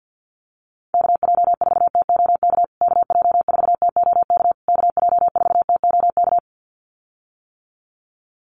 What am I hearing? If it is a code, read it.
Morse code, heard as a keyed tone, about 50 wpm, 700 Hz. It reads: KJ4TOK KJ4TOK KJ4TOK
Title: USA Call Signs Repeated 3x 50wpm